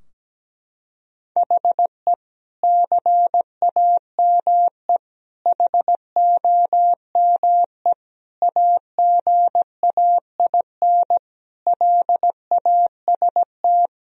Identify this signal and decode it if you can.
Morse code, heard as a keyed tone, about 17 wpm, 700 Hz. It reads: HE CAME HOME AGAIN LAST
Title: Sentences from Top 1000 Words - Review Part 2 - 17wpm